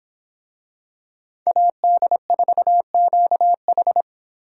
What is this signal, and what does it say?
Morse code, heard as a keyed tone, about 25 wpm, 700 Hz. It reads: AD4QH